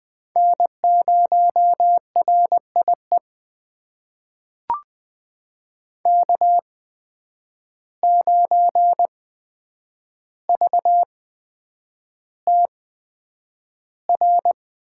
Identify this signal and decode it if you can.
Morse code, heard as a keyed tone, about 20 wpm, spaced as for 9 wpm, 700 Hz. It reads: N0RIE K9VTR
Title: USA Call Signs - 5 Characters - 8x Character Spacing - 20wpm